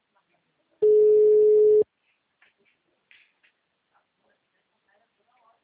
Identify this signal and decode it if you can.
Ringback tone (Europe, 425 Hz), with speech.